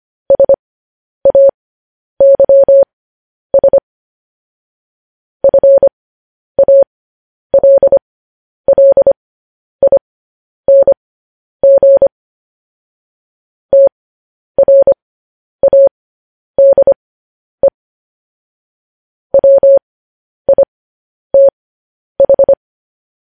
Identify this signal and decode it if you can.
Morse code, heard as a keyed tone, about 25 wpm, spaced as for 10 wpm, 550 Hz. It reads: SAYS FALLING TRADE WITH